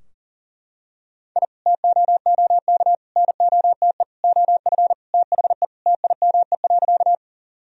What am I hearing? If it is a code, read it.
Morse code, heard as a keyed tone, about 40 wpm, 700 Hz. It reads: I TOOK NOTE OF THE TIME.
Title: Sentences from Top 1000 Words - Review Part 2 - 40wpm